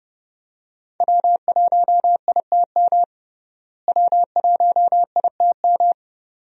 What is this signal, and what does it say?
Morse code, heard as a keyed tone, about 32 wpm, 700 Hz. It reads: W1ITM W1ITM
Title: USA Call Signs - 5 Characters - Repeated 2x - 30wpm